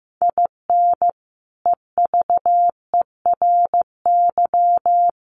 Morse code, heard as a keyed tone, about 15 wpm, 700 Hz. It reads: IN EVERY
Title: Sentences from Top 1400 Words 15wpm